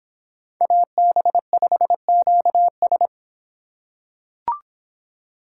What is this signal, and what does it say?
Morse code, heard as a keyed tone, about 26 wpm, 700 Hz. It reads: AB5QS